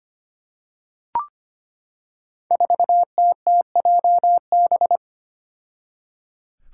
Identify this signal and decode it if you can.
Morse code, heard as a keyed tone, about 25 wpm, 700 Hz. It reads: 4TTJB
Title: Five Challenging Letter-Number Combinations 25wpm